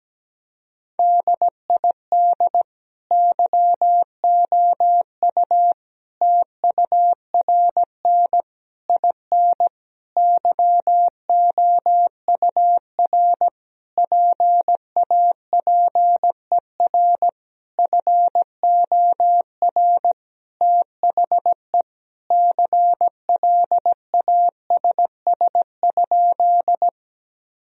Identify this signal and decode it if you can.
Morse code, heard as a keyed tone, about 17 wpm, 700 Hz. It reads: DID YOU TURN IN YOUR PAPER FOR THE CLASS?
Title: Sentences from Top 1000 Words - Review Part 2 - 17wpm